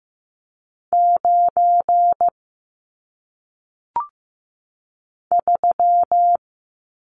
Morse code, heard as a keyed tone, about 15 wpm, 700 Hz. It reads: 9 3
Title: Numbers - 1-Digit 15wpm